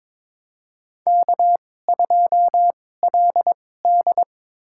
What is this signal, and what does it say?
Morse code, heard as a keyed tone, about 22 wpm, spaced as for 16 wpm, 700 Hz. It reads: K2LD
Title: USA Call Signs - 4 Characters - 2x Character Spacing - 22wpm